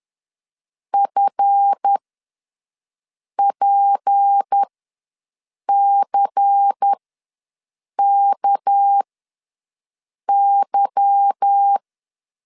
Morse code, heard as a keyed tone, about 10 wpm, spaced as for 5 wpm, 800 Hz. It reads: FPCKY
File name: LEZIONI / 9 2 ws 10wpm 4130